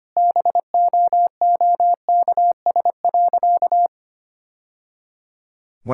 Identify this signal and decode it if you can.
Morse code, heard as a keyed tone, about 25 wpm, 700 Hz. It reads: BOOKS.
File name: practice / 25wpm / Sentences with 3-Word Phrases - 25wpm